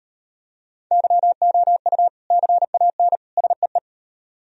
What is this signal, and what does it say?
Morse code, heard as a keyed tone, about 38 wpm, 700 Hz. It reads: YOU CAN SEE